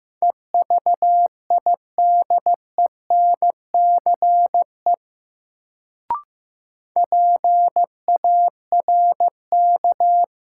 Morse code, heard as a keyed tone, about 15 wpm, 700 Hz. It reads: EVIDENCE PARK
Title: Sentences from Top 1300 Words - New Word Drill - 15wpm